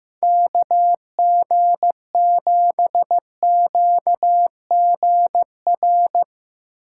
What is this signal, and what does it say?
Morse code, heard as a keyed tone, about 15 wpm, 700 Hz. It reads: KG7QGR
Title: USA Call Signs - 4x Character Spacing - 15wpm